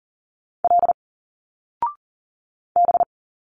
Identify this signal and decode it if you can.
Morse code, heard as a keyed tone, about 39 wpm, 700 Hz. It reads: L B
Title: Single Letters 40wpm